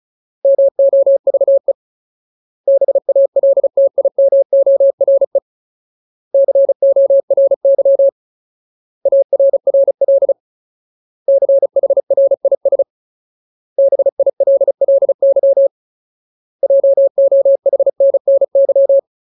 Morse code, heard as a keyed tone, about 34 wpm, 550 Hz. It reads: MOVE BALTIMORE CORY ARRL CHRIS BILLY JOHNNY
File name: Ultimate Word List / 35WPM Ultimate